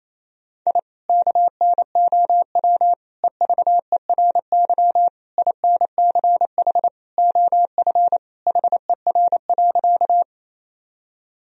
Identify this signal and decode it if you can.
Morse code, heard as a keyed tone, about 28 wpm, 700 Hz. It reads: I KNOW EVERY INCH OF HER.
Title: Sentences from Top 1000 Words - Review Part 3 - 28wpm